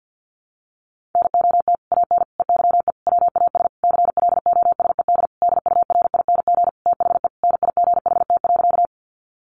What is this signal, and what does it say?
Morse code, heard as a keyed tone, about 50 wpm, 700 Hz. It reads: NOT AN EYE WAS CLOSED DURING THE NIGHT.
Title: Sentences from Top 1400 Words 50wpm